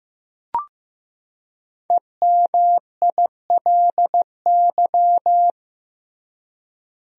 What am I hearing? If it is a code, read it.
Morse code, heard as a keyed tone, about 15 wpm, 700 Hz. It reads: EMILY
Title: USA Common Names - 15wpm